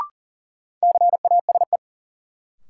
Morse code, heard as a keyed tone, about 40 wpm, 700 Hz. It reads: CASE
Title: Sets of 2 Words - New Word Drill - 40wpm